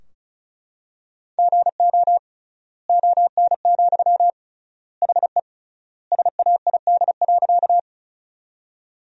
Morse code, heard as a keyed tone, about 36 wpm, 700 Hz. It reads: GO ON, HE SAID.
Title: 3x Word-Spacing - Sentences from Top 100 Words - 35wpm